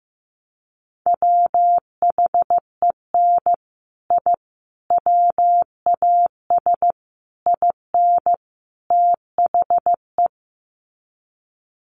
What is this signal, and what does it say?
Morse code, heard as a keyed tone, about 15 wpm, 700 Hz. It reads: WHEN I WAS IN THE
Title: Sets of 5 Words Encore2 15wpm